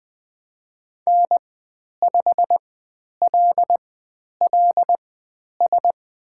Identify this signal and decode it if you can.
Morse code, heard as a keyed tone, about 20 wpm, spaced as for 10 wpm, 700 Hz. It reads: N5LLS
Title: 095 Lesson 16g - Letter c - Call Signs 20wpm